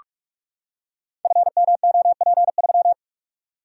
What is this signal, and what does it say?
Morse code, heard as a keyed tone, about 44 wpm, 690 Hz. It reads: FMOP3
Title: Code Groups - 5 Letter:Numbers - 45wpm